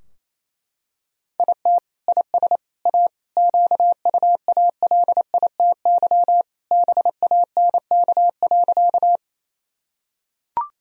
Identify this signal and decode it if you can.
Morse code, heard as a keyed tone, about 28 wpm, 700 Hz. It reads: IT IS A QUALITY BANK.
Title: Sentences from Top 1000 Words - Review Part 4 - 28wpm